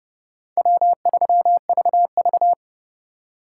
Morse code, heard as a keyed tone, about 36 wpm, 700 Hz. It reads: W3VV
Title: Speed Racing - USA 1x2 Call Signs 30wpm